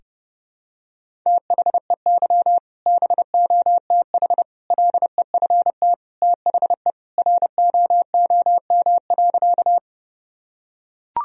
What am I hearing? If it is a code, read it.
Morse code, heard as a keyed tone, about 30 wpm, 700 Hz. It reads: THEY BOTH LEFT THE ROOM.
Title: Sentences from Top 1000 Words - Review Part 1 - 30wpm